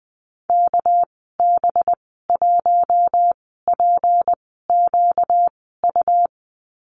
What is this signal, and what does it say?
Morse code, heard as a keyed tone, about 20 wpm, spaced as for 14 wpm, 700 Hz. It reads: KB1PQU
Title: 238 Lesson 38g - Letter q - Call Signs 20wpm